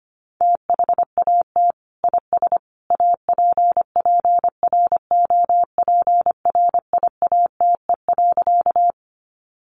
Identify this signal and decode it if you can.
Morse code, heard as a keyed tone, about 25 wpm, 700 Hz. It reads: THAT IS APPROPRIATE.
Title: Sentences from Top 900 Words 25wpm